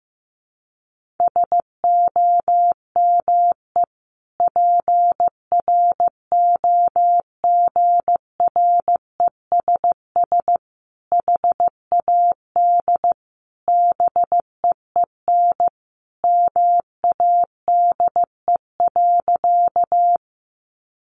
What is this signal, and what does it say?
Morse code, heard as a keyed tone, about 15 wpm, 700 Hz. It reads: SOME PROGRESS HAD BEEN MADE.
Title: Sentences from Top 1400 Words 15wpm